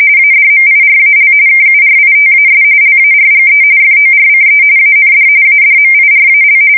RTTY, about 45.45 baud. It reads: ADMIRAL KURITAS LARGE FORCE AND ADMIRAL
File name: fsk170 / USS-New-Jersey